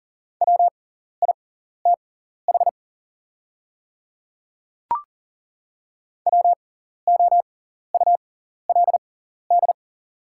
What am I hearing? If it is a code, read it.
Morse code, heard as a keyed tone, about 40 wpm, spaced as for 14 wpm, 690 Hz. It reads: WITH WOULD
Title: 6x Character Spacing - Top 300 Words - 40wpm